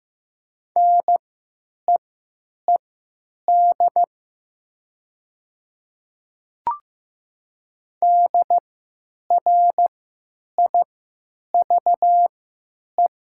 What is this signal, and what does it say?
Morse code, heard as a keyed tone, about 15 wpm, spaced as for 8 wpm, 700 Hz. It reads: NEED DRIVE
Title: 3x Character Spacing - Top 500 Words - 15wpm